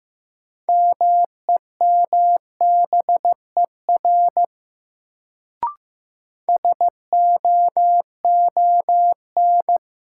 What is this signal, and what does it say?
Morse code, heard as a keyed tone, about 15 wpm, 700 Hz. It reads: MEMBER SOON
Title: Sets of 3 Words - New Word Drill - 15wpm